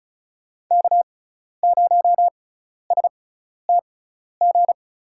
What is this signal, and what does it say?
Morse code, heard as a keyed tone, about 36 wpm, spaced as for 12 wpm, 700 Hz. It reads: K0STG
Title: USA Call Signs - 5 Characters - 6x Character Spacing - 35wpm